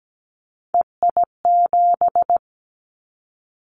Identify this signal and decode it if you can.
Morse code, heard as a keyed tone, about 17 wpm, 700 Hz. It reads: EI7
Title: Callsign Prefixes for Top 50 DXCC - 17wpm